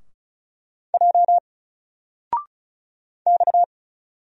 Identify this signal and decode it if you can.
Morse code, heard as a keyed tone, about 36 wpm, 700 Hz. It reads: J X
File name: Single-Letter-Number-35wpm